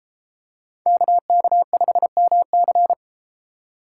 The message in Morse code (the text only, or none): KK5MC